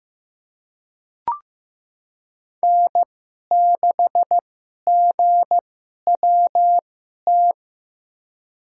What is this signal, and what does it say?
Morse code, heard as a keyed tone, about 15 wpm, spaced as for 11 wpm, 700 Hz. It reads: N6GWT
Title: USA Call Signs - 5 Characters - 2x Character Spacing - 15wpm